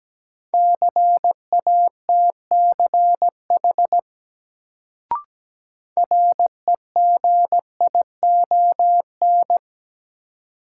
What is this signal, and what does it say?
Morse code, heard as a keyed tone, about 17 wpm, 700 Hz. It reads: CATCH REGION
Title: Sentences from Top 1500 Words - New Word Drill - 17wpm